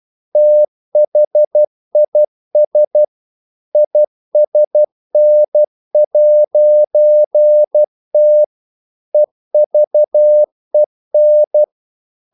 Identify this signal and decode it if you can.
Morse code, heard as a keyed tone, about 12 wpm, 600 Hz. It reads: THIS ISN'T EVEN
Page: English quotations in CW (telegraphy - morse)